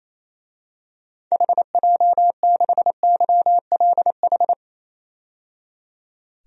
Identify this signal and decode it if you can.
Morse code, heard as a keyed tone, about 28 wpm, 700 Hz. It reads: HJ6YLH
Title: DXCC-100 Call Signs - 6 Characters - 28wpm